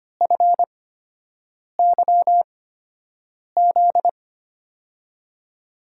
Morse code, heard as a keyed tone, about 25 wpm, spaced as for 7 wpm, 700 Hz. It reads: FYZ
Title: USA Call Signs - 6 Characters - 8x Character Spacing - 25wpm